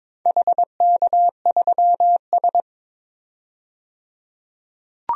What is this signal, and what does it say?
Morse code, heard as a keyed tone, about 22 wpm, 690 Hz. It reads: HK3S
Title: DXCC-100 Call Signs - 4 Characters - 22wpm